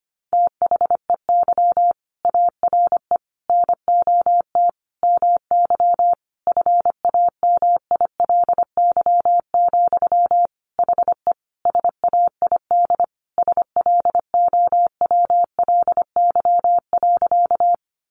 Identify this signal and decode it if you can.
Morse code, heard as a keyed tone, about 25 wpm, 700 Hz. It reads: THEY ARE NOT MY FAMILY, HE SAID SLOWLY.
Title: Sentences from Top 1100 Words 25wpm